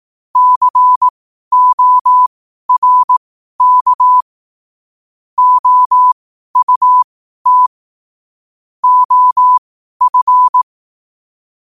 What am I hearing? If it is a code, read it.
Morse code, heard as a keyed tone, about 18 wpm, spaced as for 12 wpm, 1000 Hz. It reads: CORK OUT OF
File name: qotdfiles / 12wpm